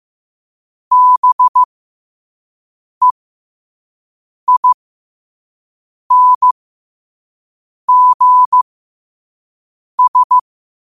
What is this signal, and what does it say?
Morse code, heard as a keyed tone, about 15 wpm, spaced as for 5 wpm, 1000 Hz. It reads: BEINGS
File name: qotdfiles / 5wpm